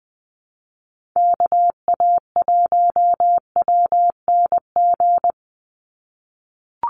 Morse code, heard as a keyed tone, about 20 wpm, 700 Hz. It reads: KA1WNG
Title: USA Call Signs 20wpm